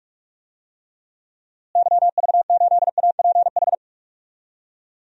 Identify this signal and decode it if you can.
Morse code, heard as a keyed tone, about 47 wpm, 700 Hz. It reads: YV8APH
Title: DXCC-100 Call Signs - 6 Characters - Repeated 3x - 45wpm